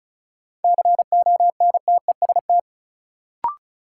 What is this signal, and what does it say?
Morse code, heard as a keyed tone, about 36 wpm, 700 Hz. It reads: CONTEST
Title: Sentences from Top 1600 Words - New Word Drill - 35wpm